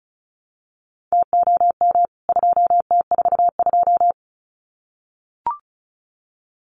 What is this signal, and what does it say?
Morse code, heard as a keyed tone, about 34 wpm, 700 Hz. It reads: TOM 2T42
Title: CWT Exchange 35wpm